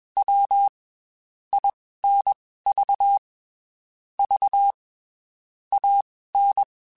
Morse code, heard as a keyed tone, about 21 wpm, spaced as for 15 wpm, 800 Hz. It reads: W INV V AN
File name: QSOs / QSOs-mp3s / WD0FBT 1@20